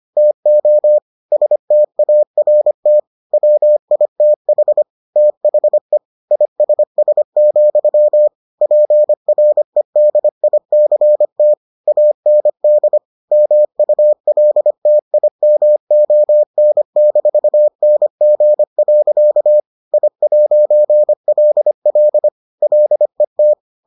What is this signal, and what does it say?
Morse code, heard as a keyed tone, about 25 wpm, 600 Hz. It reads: TO START WITH THE ISS, PREDICT AND MULTIMON-NG. I'LL LET